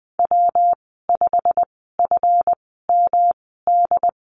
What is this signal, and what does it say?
Morse code, heard as a keyed tone, about 20 wpm, spaced as for 14 wpm, 700 Hz. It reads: W5FMD
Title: 167 Lesson 27g - Number 7 - Call Signs 20wpm